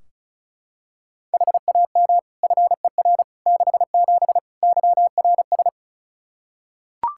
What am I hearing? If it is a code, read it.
Morse code, heard as a keyed tone, about 36 wpm, 700 Hz. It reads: HAM FER 67 YRS
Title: Practice QSO - Exchange 3 - More Info - 35wpm